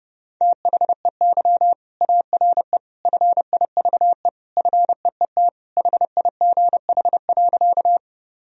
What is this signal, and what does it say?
Morse code, heard as a keyed tone, about 30 wpm, 700 Hz. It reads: THEY ARE FIVE FEET HIGH.
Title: Sentences from Top 1000 Words - Review Part 2 - 30wpm